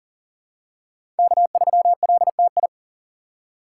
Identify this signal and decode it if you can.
Morse code, heard as a keyed tone, about 40 wpm, 700 Hz. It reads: K3LTI
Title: USA Call Signs - 3x Character Spacing - 40wpm